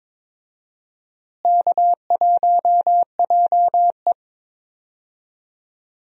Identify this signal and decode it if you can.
Morse code, heard as a keyed tone, about 21 wpm, 700 Hz. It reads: K1JE
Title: USA Call Signs - 4 Characters - Repeated 3x - 22wpm